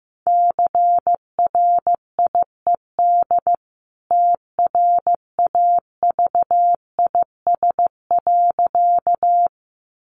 Morse code, heard as a keyed tone, about 15 wpm, 700 Hz. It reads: CRIED TRAVIS.
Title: A Sound Of Thunder 15wpm